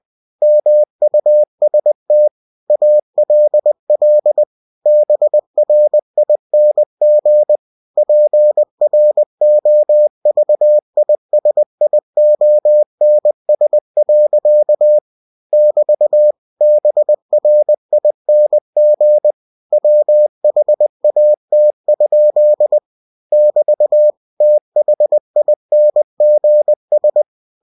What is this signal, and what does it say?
Morse code, heard as a keyed tone, about 20 wpm, 600 Hz. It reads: MUST ALL BRING PROVISIONS. <BT> BRING WHAT? <BT> THINGS
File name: cw-books / winnie-the-pooh / winnie-20wpm-page-0077